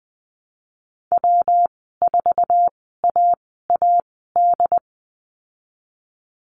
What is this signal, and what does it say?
Morse code, heard as a keyed tone, about 20 wpm, spaced as for 15 wpm, 700 Hz. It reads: W4AAD
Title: 173 Lesson 28g - Number 9 - Call Signs 20wpm